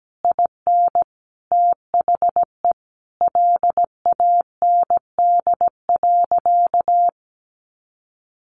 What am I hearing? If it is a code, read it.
Morse code, heard as a keyed tone, about 17 wpm, 700 Hz. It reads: IN THE LAND.
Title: Sentences from Top 1500 Words 17wpm